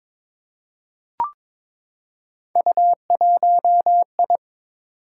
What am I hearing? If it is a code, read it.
Morse code, heard as a keyed tone, about 22 wpm, 700 Hz. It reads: U1I
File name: practice / 22wpm / Three Letter-Number Combinations - 22wpm